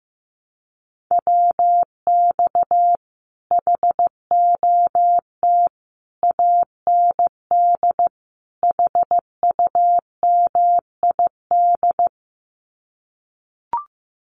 Morse code, heard as a keyed tone, about 15 wpm, 700 Hz. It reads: WX HOT AND HUMID
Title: Common QSO Phrases Encore 15wpm